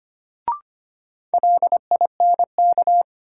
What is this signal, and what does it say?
Morse code, heard as a keyed tone, about 25 wpm, 700 Hz. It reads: LINK
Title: Sentences from Top 1500 Words - New Word Drill Alt - 25wpm